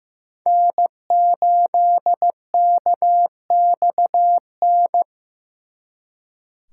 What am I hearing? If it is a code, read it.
Morse code, heard as a keyed tone, about 18 wpm, 700 Hz. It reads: N8KXN
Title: Speed Racing - USA 1x3 Call Signs 15wpm